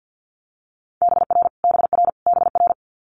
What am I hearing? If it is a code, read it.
Morse code, heard as a keyed tone, about 50 wpm, 700 Hz. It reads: BR BR BR
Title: Multi-letter Phonemes Repeated 3x 50wpm